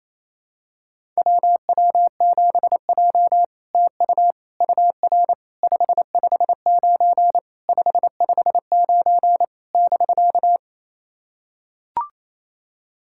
Morse code, heard as a keyed tone, about 28 wpm, 700 Hz. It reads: WW7J TU UR 559 559 <BK>
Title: SOTA QSO Practice 28wpm